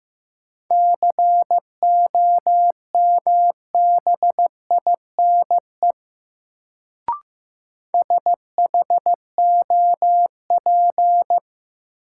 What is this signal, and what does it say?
Morse code, heard as a keyed tone, about 15 wpm, 700 Hz. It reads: COMBINE SHOP
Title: Sentences from Top 1600 Words - New Word Drill Alt - 15wpm